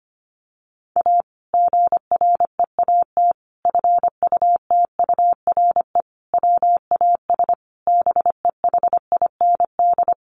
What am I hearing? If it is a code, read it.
Morse code, heard as a keyed tone, about 25 wpm, 700 Hz. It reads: A GREAT FUTURE WAS BEHIND
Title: Sentences from Top 700 Words 25wpm